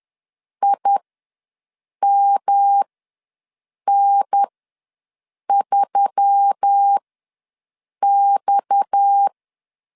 Morse code, heard as a keyed tone, about 10 wpm, spaced as for 6 wpm, 800 Hz. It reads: IMN3X
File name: LEZIONI / 6 PKX0 / 6 PKX ws 10wpm 3706